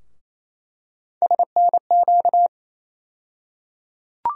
Morse code, heard as a keyed tone, about 28 wpm, 700 Hz. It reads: SNQ
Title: USA Call Sign Postfixes - 3 Letters 28wpm